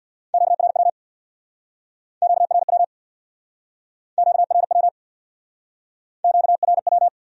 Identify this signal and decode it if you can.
Morse code, heard as a keyed tone, about 62 wpm, 700 Hz. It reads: QRW QRW QRW QRW
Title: Speed-Racing Q-Codes - 50wpm